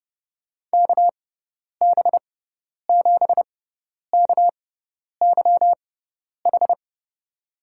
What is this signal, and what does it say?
Morse code, heard as a keyed tone, about 30 wpm, spaced as for 10 wpm, 700 Hz. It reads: KB7KYH
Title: USA Call Signs - 6 Characters - 6x Character Spacing - 30wpm